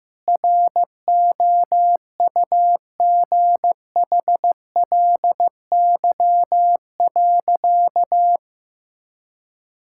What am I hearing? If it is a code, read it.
Morse code, heard as a keyed tone, about 15 wpm, 700 Hz. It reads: ROUGHLY.
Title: Sentences from Top 1600 Words 15wpm